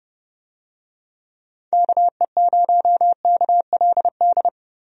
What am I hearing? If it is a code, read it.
Morse code, heard as a keyed tone, about 30 wpm, 700 Hz. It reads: KE0KLD
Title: USA Call Signs - 6 Characters - 30wpm